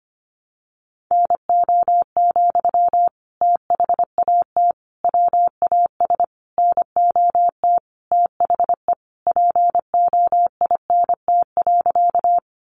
Morse code, heard as a keyed tone, about 25 wpm, 700 Hz. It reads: NO, THAT WAS NOT THE POINT.
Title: Sentences from Top 600 Words 25wpm